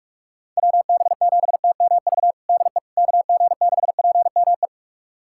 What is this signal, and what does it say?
Morse code, heard as a keyed tone, about 43 wpm, 700 Hz. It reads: WB7TMV DE KG6PME
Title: USA Callsign DE Callsign 45wpm